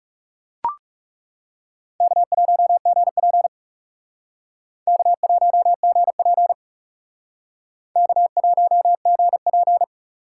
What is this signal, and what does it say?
Morse code, heard as a keyed tone, about 37 wpm, 700 Hz. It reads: K1GP K1GP K1GP
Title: Speed Racing - USA 1x2 Call Signs 30wpm